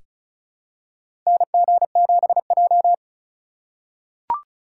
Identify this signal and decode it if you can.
Morse code, heard as a keyed tone, about 36 wpm, 700 Hz. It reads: NG7J
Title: USA Call Signs - 4 Characters - 4x Character Spacing - 35wpm